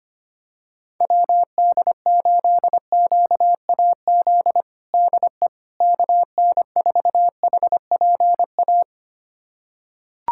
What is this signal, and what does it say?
Morse code, heard as a keyed tone, about 25 wpm, 700 Hz. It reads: WD8QAZ DE KN4HPA
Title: USA Callsign DE Callsign - 6 Characters - 25wpm